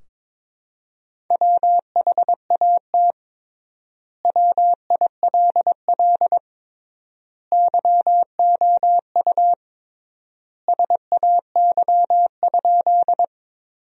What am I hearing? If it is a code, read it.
Morse code, heard as a keyed tone, about 22 wpm, 700 Hz. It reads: WHAT WILL YOU SAY?
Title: 3x Word-Spacing - Sentences from Top 200 Words - 22wpm